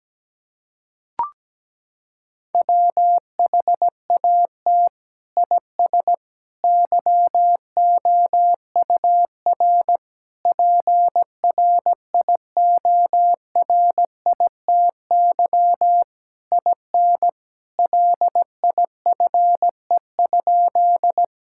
Morse code, heard as a keyed tone, about 17 wpm, 700 Hz. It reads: WHAT IS YOUR PRIORITY IN LIFE?
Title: Sentences from Top 1600 Words 17wpm